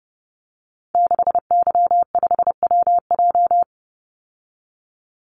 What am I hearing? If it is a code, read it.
Morse code, heard as a keyed tone, about 30 wpm, 700 Hz. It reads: 6Y5WJ
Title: Call Signs Repeated 3x - 30wpm